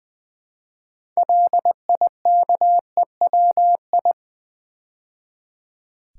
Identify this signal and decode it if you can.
Morse code, heard as a keyed tone, about 20 wpm, 690 Hz. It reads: LIKEWI
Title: Build A Word - 3 Letters At A Time - 8-Letter Words - 20wpm